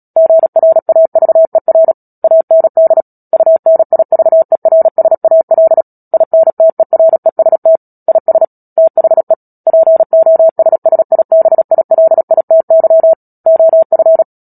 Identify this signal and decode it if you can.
Morse code, heard as a keyed tone, about 37 wpm, 650 Hz. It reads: GRAVER AND UNIVERSAL INTEREST IS THE POSSIBILITY OF
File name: Book Two / 10-Chapter 10 - Epilogue - 36 WPM